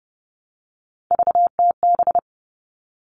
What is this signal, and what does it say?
Morse code, heard as a keyed tone, about 30 wpm, 700 Hz. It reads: VTB
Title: All Cut Numbers - 3-Digit 30wpm